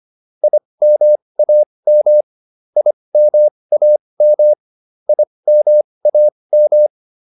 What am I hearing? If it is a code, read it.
Morse code, heard as a keyed tone, about 25 wpm, spaced as for 20 wpm, 600 Hz. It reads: IMAM IMAM IMAM